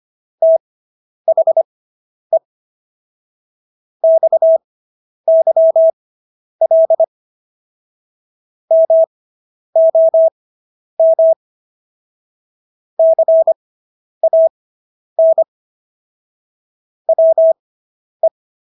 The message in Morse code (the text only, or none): THE XYL MOM CAN WE